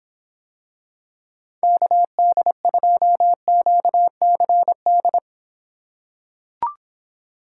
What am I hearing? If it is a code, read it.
Morse code, heard as a keyed tone, about 26 wpm, 700 Hz. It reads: KD2QCD